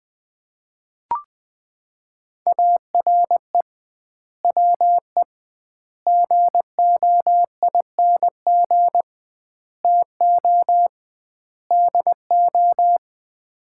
Morse code, heard as a keyed tone, about 20 wpm, 700 Hz. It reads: ARE WE GOING TO DO
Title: 2x Word-Spacing - Sets of 5 Words - 20wpm